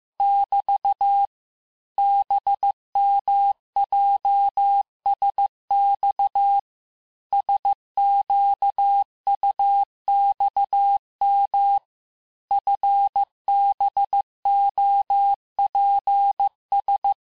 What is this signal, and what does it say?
Morse code, heard as a keyed tone, about 15 wpm, 800 Hz. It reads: <BT> BMJSX SQUXM FBOPS
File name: cours-du-lundi files / cours-1585-texte-chiffre-vitesse-14mpm